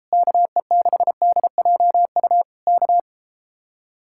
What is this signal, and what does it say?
Morse code, heard as a keyed tone, about 38 wpm, 700 Hz. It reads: KE6DJU K